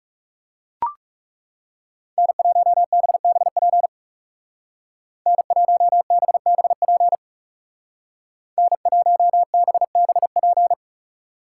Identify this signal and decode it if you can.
Morse code, heard as a keyed tone, about 35 wpm, 700 Hz. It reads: N1BBP N1BBP N1BBP
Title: Speed Racing - USA 1x3 Call Signs 30wpm